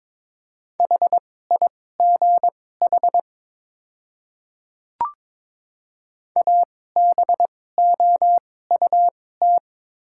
Morse code, heard as a keyed tone, about 22 wpm, spaced as for 16 wpm, 700 Hz. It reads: HIGH ABOUT
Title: 2x Character Spacing - Top 200 Words - 22wpm